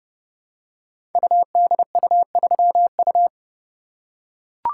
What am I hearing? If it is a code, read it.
Morse code, heard as a keyed tone, about 28 wpm, 700 Hz. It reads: UDU3U
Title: Code Groups - 5 Letter:Numbers - 30wpm